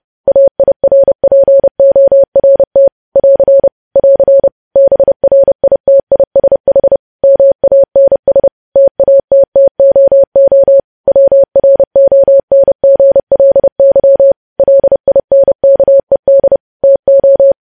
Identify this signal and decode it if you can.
Morse code, heard as a keyed tone, about 30 wpm, 550 Hz. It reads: AIRPORT <AR> <AR> BRITISH MANS TATTOO WRONGLY LINKED TO
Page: news headlines in morse code at 30 wpm